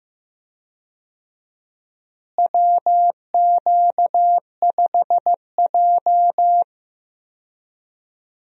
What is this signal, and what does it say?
Morse code, heard as a keyed tone, about 15 wpm, 700 Hz. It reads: WQ5J
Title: USA Call Signs - 4 Characters - Repeated 2x - 15wpm